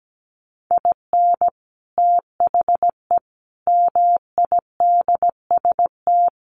Morse code, heard as a keyed tone, about 17 wpm, 700 Hz. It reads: IN THE MIDST
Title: Sets of 5 Words 17wpm